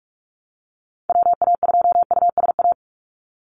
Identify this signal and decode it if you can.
Morse code, heard as a keyed tone, about 46 wpm, 700 Hz. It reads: WA2USA
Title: NA SOTA Activator Callsigns 45wpm